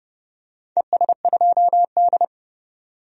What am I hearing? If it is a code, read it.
Morse code, heard as a keyed tone, about 32 wpm, 700 Hz. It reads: ES2D
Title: DXCC-100 Call Signs - 4 Characters - 30wpm